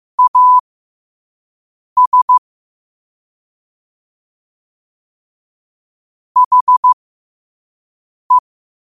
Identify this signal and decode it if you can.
Morse code, heard as a keyed tone, about 15 wpm, spaced as for 5 wpm, 1000 Hz. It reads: AS HE
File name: qotdfiles / 5wpm